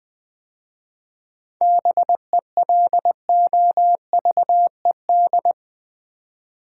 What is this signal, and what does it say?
Morse code, heard as a keyed tone, about 20 wpm, 700 Hz. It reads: BELOVED
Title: 7 Letter Words - First 3 Letters Given - 20wpm